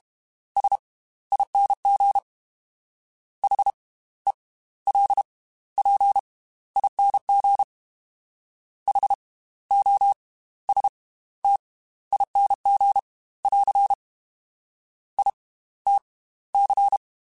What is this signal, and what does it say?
Morse code, heard as a keyed tone, about 32 wpm, spaced as for 16 wpm, 800 Hz. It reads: SING HELPING HOSTING <AR> ITC